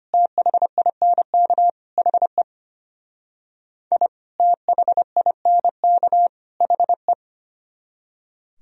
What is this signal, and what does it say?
Morse code, heard as a keyed tone, about 32 wpm, 700 Hz. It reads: THINK HE I THINK HE